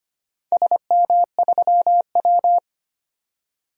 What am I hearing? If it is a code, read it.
Morse code, heard as a keyed tone, about 25 wpm, 700 Hz. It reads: SM3W
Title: DXCC-100 Call Signs - 4 Characters - 25wpm